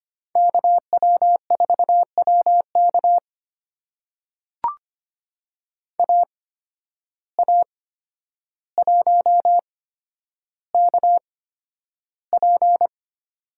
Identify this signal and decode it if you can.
Morse code, heard as a keyed tone, about 25 wpm, spaced as for 11 wpm, 700 Hz. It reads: KW4WK AA1KP
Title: USA Call Signs - 5 Characters - 8x Character Spacing - 25wpm